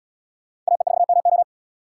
Morse code, heard as a keyed tone, about 77 wpm, 700 Hz. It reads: AE4MO